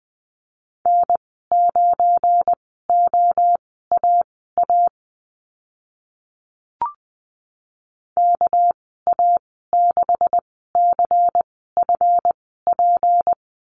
Morse code, heard as a keyed tone, about 20 wpm, spaced as for 15 wpm, 700 Hz. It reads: N9OAA KA6CFP
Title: 200 Lesson 32g - Letter k - Call Signs 20wpm